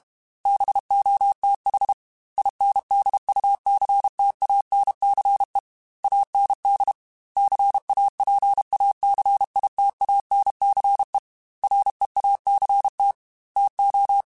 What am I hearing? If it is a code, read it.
Morse code, heard as a keyed tone, about 32 wpm, 800 Hz. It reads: BOTH INDUCTANCE AND CAPACITANCE REACT TO
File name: Technical Subjects / Technical Subjects-mp3s / Reactance @30